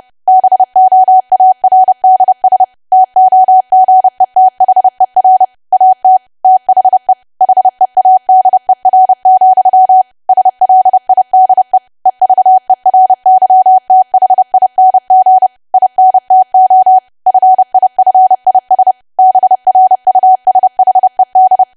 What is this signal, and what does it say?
Morse code, heard as a keyed tone, about 30 wpm, 750 Hz. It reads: BOARDS TOGETHER AT THE HEADER, SLIDE EVERYTHING INTO FIFIS BRUSHED
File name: cwa / W1AW-Level3 / 30WPM